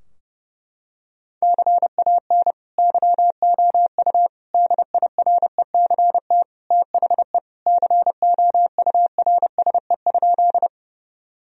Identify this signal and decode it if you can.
Morse code, heard as a keyed tone, about 32 wpm, 700 Hz. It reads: CAN YOU DIRECT THE COURSE?
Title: Sentences from Top 1000 Words - Review Part 2 - 30wpm